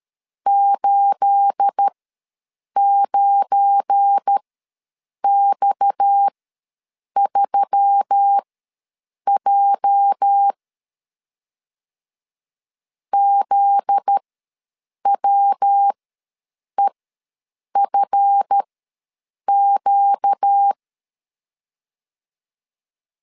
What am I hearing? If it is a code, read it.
Morse code, heard as a keyed tone, about 13 wpm, spaced as for 7 wpm, 800 Hz. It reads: 89X3J ZWEFQ
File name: LEZIONI / 9 28 ws 12wpm 3354